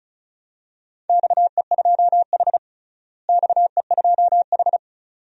Morse code, heard as a keyed tone, about 34 wpm, 700 Hz. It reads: XE2H XE2H
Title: DXCC-100 Call Signs - 4 Characters - Repeated 2x - 35wpm